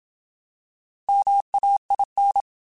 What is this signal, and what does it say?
Morse code, heard as a keyed tone, about 27 wpm, 800 Hz. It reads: MAIN
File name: Alphabetical / Alphabetical mp3s / Ms @25